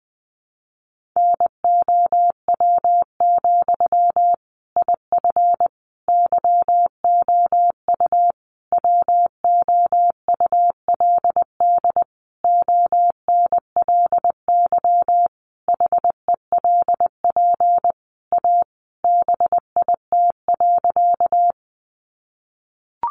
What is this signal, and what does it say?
Morse code, heard as a keyed tone, about 20 wpm, 700 Hz. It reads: NOW, IF YOU WOULD ONLY HELP A BIT.
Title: Sentences from Top 900 Words 20wpm